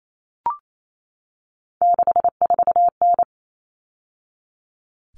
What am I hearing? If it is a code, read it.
Morse code, heard as a keyed tone, about 28 wpm, 700 Hz. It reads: 64N
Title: All Cut Numbers - 3-Digit 28wpm